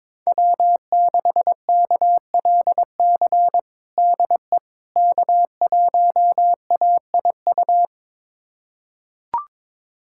Morse code, heard as a keyed tone, about 22 wpm, 700 Hz. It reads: W6KLC DE K1AIU
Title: USA Callsign DE Callsign - 5 Characters - 22wpm